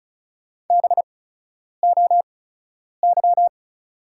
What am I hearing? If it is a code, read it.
Morse code, heard as a keyed tone, about 34 wpm, spaced as for 10 wpm, 690 Hz. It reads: BOY
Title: 8x Character Spacing - Top 200 Words - 35wpm